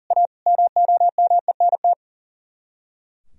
Morse code, heard as a keyed tone, about 48 wpm, 700 Hz. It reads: A MOMENT
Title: Speed-Racing - Sets of 5 Words 40wpm